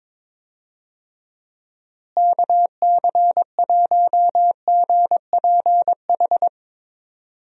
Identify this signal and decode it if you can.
Morse code, heard as a keyed tone, about 22 wpm, 700 Hz. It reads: KC1GPH
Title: USA Call Signs - 6 Characters - Repeated 3x - 22wpm